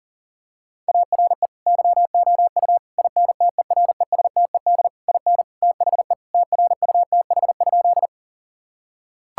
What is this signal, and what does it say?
Morse code, heard as a keyed tone, about 40 wpm, 700 Hz. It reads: ARE YOU INTERESTED IN THE TRUTH?
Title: Sentences from Top 1000 Words - Review Part 6 - 40wpm